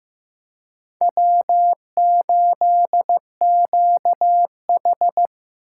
Morse code, heard as a keyed tone, about 15 wpm, 700 Hz. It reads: W8QH
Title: USA Call Signs - 4 Characters - 2x Character Spacing - 15wpm